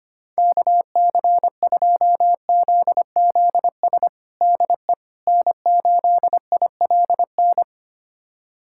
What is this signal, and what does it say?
Morse code, heard as a keyed tone, about 25 wpm, 700 Hz. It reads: KC2ZZS DE N8ILN
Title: USA Callsign DE Callsign 25wpm